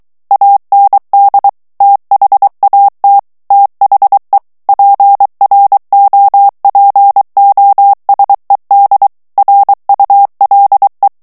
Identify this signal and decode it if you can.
Morse code, heard as a keyed tone, about 23 wpm, 800 Hz. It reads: AND THAT THE PROPOSED RULE
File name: ANGLAIS texte clair 3 vitesse 22 mots minute